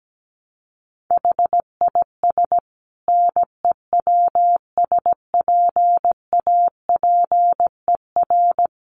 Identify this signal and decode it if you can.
Morse code, heard as a keyed tone, about 17 wpm, 700 Hz. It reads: HIS NEWSPAPER
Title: Sentences from Top 1100 Words 17wpm